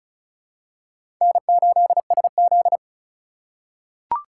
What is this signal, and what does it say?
Morse code, heard as a keyed tone, about 36 wpm, 700 Hz. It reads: N8SZ
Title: USA Call Signs - 4 Characters - 4x Character Spacing - 35wpm